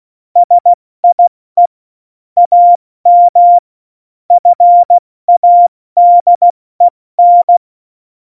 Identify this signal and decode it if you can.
Morse code, heard as a keyed tone, about 16 wpm, 700 Hz. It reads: SIE AM FADEN
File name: Morsekurs / Witze / Witz 20. November 2018, 70 F80